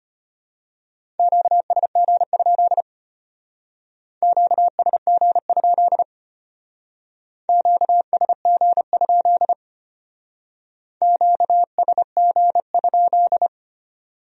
Morse code, heard as a keyed tone, about 32 wpm, 700 Hz. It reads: QSG? QSG? QSG? QSG?